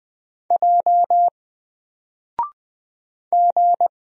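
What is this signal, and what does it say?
Morse code, heard as a keyed tone, about 20 wpm, 700 Hz. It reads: J G
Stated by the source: Single letter and number Morse drill at 20 wpm, from N4LCD.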